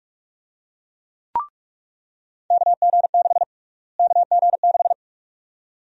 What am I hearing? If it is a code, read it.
Morse code, heard as a keyed tone, about 46 wpm, 690 Hz. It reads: KG6 KG6
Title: USA Call Sign Prefixes - Repeated 2x - 45wpm